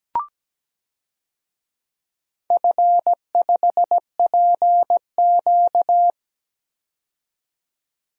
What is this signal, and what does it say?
Morse code, heard as a keyed tone, about 17 wpm, 700 Hz. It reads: F5PQ
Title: DXCC-100 Call Signs - 4 Characters - 17wpm